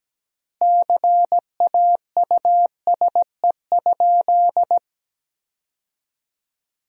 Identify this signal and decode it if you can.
Morse code, heard as a keyed tone, about 17 wpm, 700 Hz. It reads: CAUSE?
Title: 4x Word-Spacing - Sentences from Top 200 Words - 17wpm